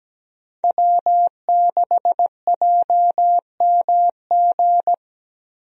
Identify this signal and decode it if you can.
Morse code, heard as a keyed tone, about 17 wpm, 700 Hz. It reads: W6JMG
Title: USA Call Signs - 5 Characters - Repeated 2x - 17wpm